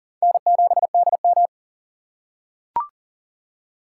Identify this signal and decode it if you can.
Morse code, heard as a keyed tone, about 40 wpm, 700 Hz. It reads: N7DM
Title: USA Call Signs - 4 Characters - 3x Character Spacing - 40wpm